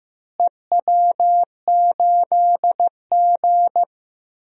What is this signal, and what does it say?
Morse code, heard as a keyed tone, about 15 wpm, 700 Hz. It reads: EW8G
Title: DXCC-100 Call Signs - 4 Characters - Repeated 3x - 15wpm